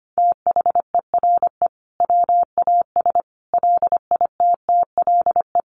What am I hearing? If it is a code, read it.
Morse code, heard as a keyed tone, about 25 wpm, 700 Hz. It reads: THERE WAS LITTLE
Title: Sentences from Top 1400 Words 25wpm